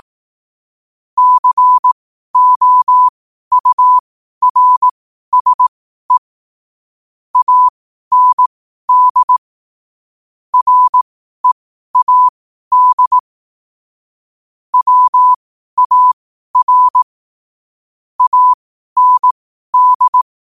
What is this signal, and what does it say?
Morse code, heard as a keyed tone, about 18 wpm, spaced as for 12 wpm, 1000 Hz. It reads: COURSE AND READ WAR AND
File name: qotdfiles / 12wpm